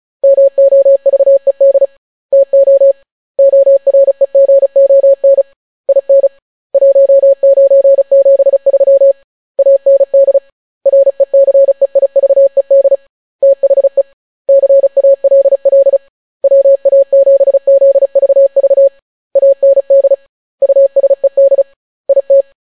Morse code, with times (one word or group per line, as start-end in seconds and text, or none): 0.2-1.9 MOVED
2.3-2.9 TO
3.4-5.4 OREGON
5.9-6.3 IN
6.7-9.1 1973
9.6-10.4 AND
10.9-12.9 RECEIVED
13.4-14.0 THE
14.5-16.0 CALL
16.4-18.9 WA7ZVV
19.4-20.1 AND
20.6-21.6 USED
22.1-22.4 IT